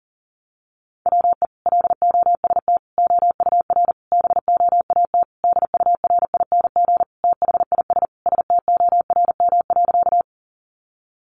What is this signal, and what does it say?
Morse code, heard as a keyed tone, about 40 wpm, 700 Hz. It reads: WE LOST OUR BOAT DURING THIS STORM.
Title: Sentences from Top 1500 Words 40wpm